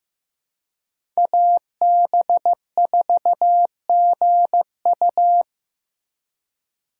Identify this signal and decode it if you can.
Morse code, heard as a keyed tone, about 15 wpm, 700 Hz. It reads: AB4GU
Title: USA Call Signs - 5 Characters - 2x Character Spacing - 15wpm